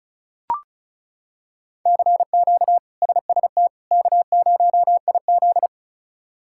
Morse code, heard as a keyed tone, about 36 wpm, 700 Hz. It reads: CQ SST K0IZ
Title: K1USN SST QSO Practice - 35wpm